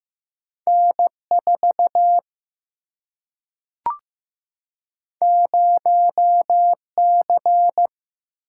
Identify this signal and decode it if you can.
Morse code, heard as a keyed tone, about 15 wpm, 700 Hz. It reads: N 4 0 C
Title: Single-Letter Single-Number 15wpm